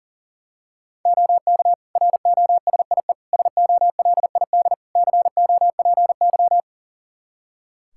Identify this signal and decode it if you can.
Morse code, heard as a keyed tone, about 40 wpm, 700 Hz. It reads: OK ROSIE SOLID COPY
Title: Practice QSO - Exchange 3 - More Info - 40wpm